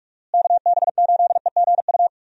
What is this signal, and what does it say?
Morse code, heard as a keyed tone, about 46 wpm, 700 Hz. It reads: KB8EGU
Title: USA Call Signs - 6 Characters - Repeated 3x - 45wpm